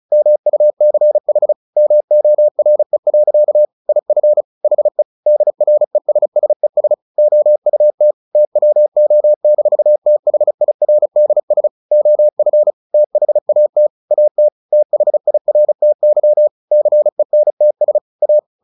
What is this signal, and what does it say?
Morse code, heard as a keyed tone, about 34 wpm, 600 Hz. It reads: MUCH MORE. IF HE DRESSES OUT TWO-THIRDS OF THAT AT THIRTY CENTS A
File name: the old man and the sea 2 35wpm